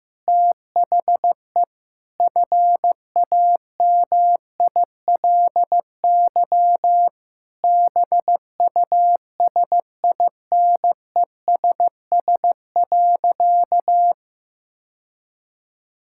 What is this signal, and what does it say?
Morse code, heard as a keyed tone, about 15 wpm, 700 Hz. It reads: THE FAMILY BUSINESS.
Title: Sentences from Top 1000 Words - Review Part 6 - 15wpm